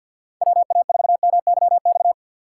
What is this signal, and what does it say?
Morse code, heard as a keyed tone, about 52 wpm, 700 Hz. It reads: WA4MYX
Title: USA Call Signs - 6 Characters - 3x Character Spacing - 50wpm